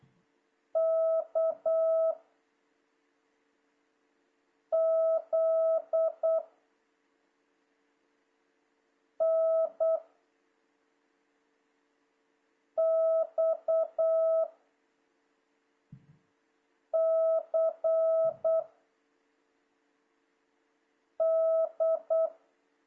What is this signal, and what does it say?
Morse code, heard as a keyed tone, about 8 wpm, 660 Hz. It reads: K Z N X C D
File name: jh8cbh / 2026cw